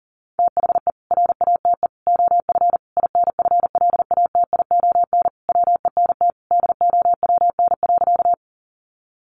Morse code, heard as a keyed tone, about 40 wpm, 700 Hz. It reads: THE RATE OF INFLATION WENT DOWN.
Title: Sentences from Top 1100 Words 40wpm